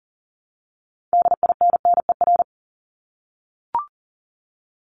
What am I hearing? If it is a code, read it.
Morse code, heard as a keyed tone, about 40 wpm, 700 Hz. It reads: DINNER
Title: Top 1751-2000 Words 40wpm